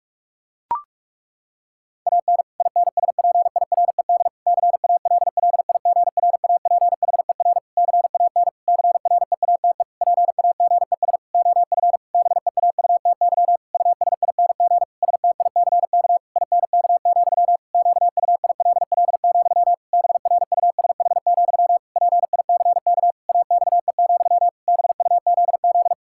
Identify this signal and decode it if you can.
Morse code, heard as a keyed tone, about 44 wpm, 700 Hz. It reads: AN INSPIRED CALLIGRAPHER CAN CREATE PAGES OF BEAUTY USING STICK INK, QUILL, BRUSH, PICK AXE, BUZZ
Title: Pangram Challenge v2 - 45wpm